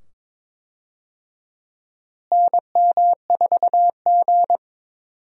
Morse code, heard as a keyed tone, about 22 wpm, 700 Hz. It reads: NM4G
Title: USA Call Signs - 4 Characters - Repeated 2x - 22wpm